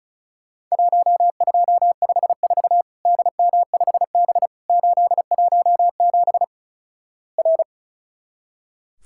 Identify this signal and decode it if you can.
Morse code, heard as a keyed tone, about 36 wpm, 700 Hz. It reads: 1254 DM5B 817 R
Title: Worked All Europe DX Contest - QTC Exchange - 35wpm